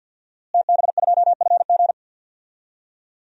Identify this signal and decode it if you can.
Morse code, heard as a keyed tone, about 51 wpm, 700 Hz. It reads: TB2FG
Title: DXCC-100 Call Signs - 5 Characters - Repeated 3x - 50wpm